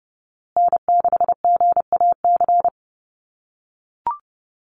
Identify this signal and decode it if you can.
Morse code, heard as a keyed tone, about 30 wpm, 700 Hz. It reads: N6GAC
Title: USA Call Signs Repeated 3x 30wpm